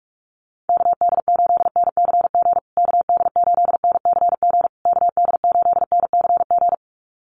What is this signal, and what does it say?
Morse code, heard as a keyed tone, about 46 wpm, 690 Hz. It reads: KD8NCG KD8NCG KD8NCG
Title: USA Call Signs Repeated 3x 45wpm